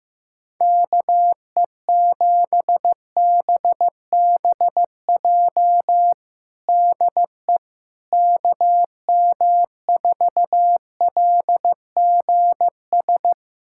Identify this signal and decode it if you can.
Morse code, heard as a keyed tone, about 15 wpm, 700 Hz. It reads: KE7BBJ DE KM4LGS
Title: USA Callsign DE Callsign 15wpm